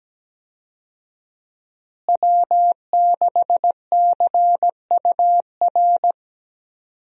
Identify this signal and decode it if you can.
Morse code, heard as a keyed tone, about 17 wpm, 700 Hz. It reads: W6CUR
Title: USA Call Signs - 5 Characters - Repeated 2x - 17wpm